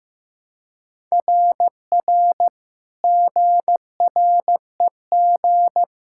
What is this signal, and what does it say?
Morse code, heard as a keyed tone, about 15 wpm, 700 Hz. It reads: RR GREG
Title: SOTA QSO Practice 15wpm